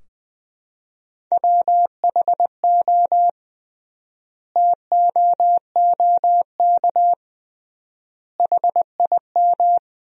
Morse code, heard as a keyed tone, about 21 wpm, 700 Hz. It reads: WHO TOOK HIM
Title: 3x Word-Spacing - Sentences from Top 300 Words - 20wpm